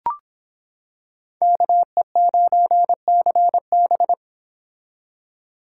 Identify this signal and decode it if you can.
Morse code, heard as a keyed tone, about 26 wpm, 700 Hz. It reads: KE9CB